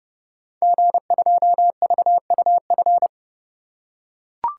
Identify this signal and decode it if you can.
Morse code, heard as a keyed tone, about 30 wpm, 700 Hz. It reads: G2VUF
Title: Code Groups - 5 Letter:Numbers - 30wpm